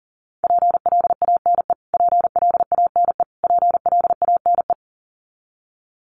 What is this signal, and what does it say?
Morse code, heard as a keyed tone, about 40 wpm, 700 Hz. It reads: PLANE PLANE PLANE
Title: Top 500 Words Repeated 40wpm